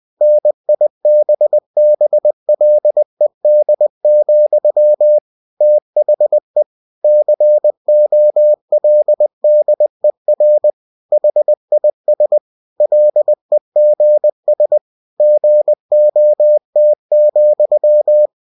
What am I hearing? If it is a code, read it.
Morse code, heard as a keyed tone, about 20 wpm, 600 Hz. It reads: NIBBLED, THE COLDER HIS LEGS GOT,
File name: cw-books / winnie-the-pooh / winnie-20wpm-page-0094